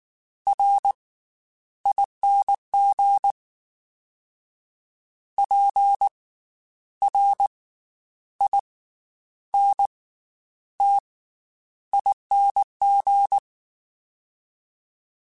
Morse code, heard as a keyed tone, about 19 wpm, 800 Hz. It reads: R ING P R I N T ING
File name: Special INGs / Special INGs-mp3s / Slow ing @18